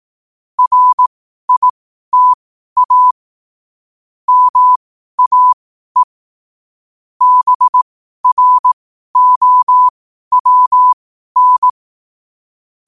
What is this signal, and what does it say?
Morse code, heard as a keyed tone, about 18 wpm, spaced as for 12 wpm, 1000 Hz. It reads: RITA MAE BROWN
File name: qotdfiles / 12wpm